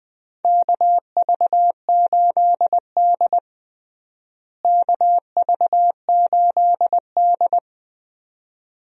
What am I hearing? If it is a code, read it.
Morse code, heard as a keyed tone, about 20 wpm, 700 Hz. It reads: KV8D KV8D
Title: USA Call Signs - 4 Characters - Repeated 2x - 20wpm